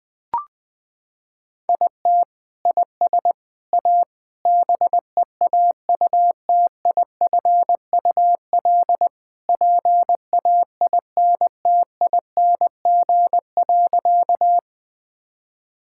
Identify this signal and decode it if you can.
Morse code, heard as a keyed tone, about 20 wpm, 700 Hz. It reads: IT IS A BEAUTIFUL PAINTING.
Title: Sentences from Top 1000 Words - Review Part 6 - 20wpm